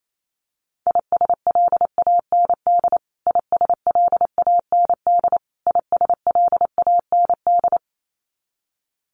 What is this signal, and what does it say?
Morse code, heard as a keyed tone, about 28 wpm, 700 Hz. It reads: ISLAND ISLAND ISLAND
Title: Top 500 Words Repeated 28wpm